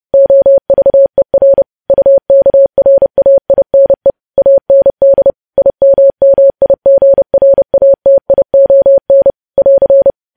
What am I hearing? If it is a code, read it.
Morse code, heard as a keyed tone, about 30 wpm, 550 Hz. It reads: OVER UKRAINE AND IMMIGRATION <AR>